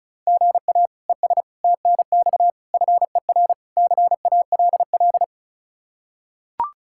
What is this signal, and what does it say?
Morse code, heard as a keyed tone, about 34 wpm, 700 Hz. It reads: GA ES TNX FER CALL
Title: K1USN SST QSO Practice 35wpm